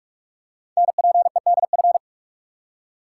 Morse code, heard as a keyed tone, about 44 wpm, 690 Hz. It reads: NPEDF
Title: Code Groups - 5 Letter:Numbers - 45wpm